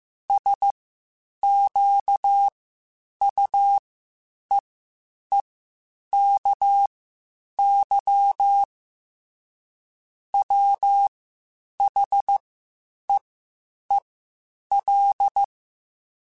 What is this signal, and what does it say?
Morse code, heard as a keyed tone, about 15 wpm, spaced as for 8 wpm, 800 Hz. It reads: SQUEEKY WHEEL
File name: Spaced / Chuckles Spaced-mp3s / Squeaky sp @13